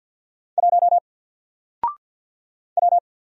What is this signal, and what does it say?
Morse code, heard as a keyed tone, about 50 wpm, 700 Hz. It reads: 1 W
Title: Single Letter-Number Alt 50wpm